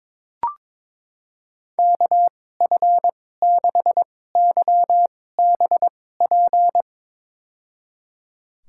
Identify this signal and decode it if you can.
Morse code, heard as a keyed tone, about 21 wpm, spaced as for 16 wpm, 700 Hz. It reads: KF6YBP
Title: USA Call Signs - 2x Character Spacing - 22wpm